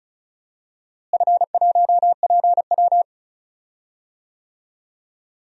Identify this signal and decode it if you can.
Morse code, heard as a keyed tone, about 36 wpm, 690 Hz. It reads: F1PW
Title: DXCC-100 Call Signs - 4 Characters - Repeated 2x - 35wpm